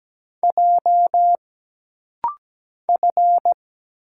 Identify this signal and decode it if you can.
Morse code, heard as a keyed tone, about 17 wpm, 700 Hz. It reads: J F
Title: Single Letters - Rapid-Fire 17wpm